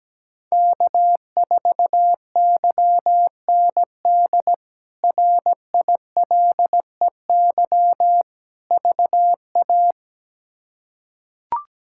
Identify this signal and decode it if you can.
Morse code, heard as a keyed tone, about 17 wpm, 700 Hz. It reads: K4YND RILEY VA
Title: K1USN SST QSO Practice - 17wpm